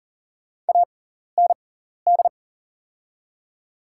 Morse code, heard as a keyed tone, about 40 wpm, spaced as for 14 wpm, 700 Hz. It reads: AND
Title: 6x Character Spacing - Top 100 Words - 40wpm